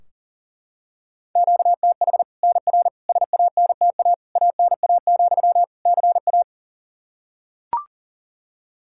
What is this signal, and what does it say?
Morse code, heard as a keyed tone, about 40 wpm, 700 Hz. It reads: QTH NR SANTA ANA, CA
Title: Practice QSO - Exchange 1 - Beginning - 40wpm